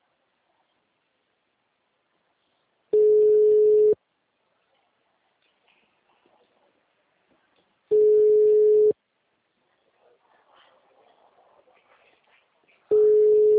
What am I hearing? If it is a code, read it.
Ringback tone (Europe, 425 Hz).